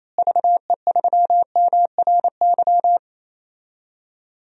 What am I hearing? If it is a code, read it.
Morse code, heard as a keyed tone, about 28 wpm, 700 Hz. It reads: VE3MRY
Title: DXCC-100 Call Signs - 6 Characters - Repeated 3x - 28wpm